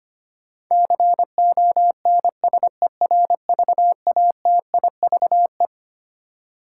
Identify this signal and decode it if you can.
Morse code, heard as a keyed tone, about 25 wpm, 700 Hz. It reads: CONSERVATIVE
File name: Twelve Letter Words - 25wpm